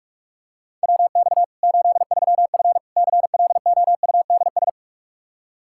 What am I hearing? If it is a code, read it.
Morse code, heard as a keyed tone, about 45 wpm, 700 Hz. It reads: WX 83F CLOUDS
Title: Practice QSO - Exchange 2 - Info - 45wpm